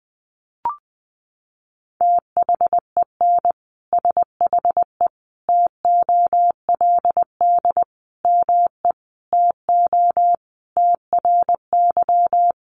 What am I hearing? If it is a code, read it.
Morse code, heard as a keyed tone, about 20 wpm, 700 Hz. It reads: THEN SHE TOLD ME TO TRY
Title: Sentences from Top 400 Words 20wpm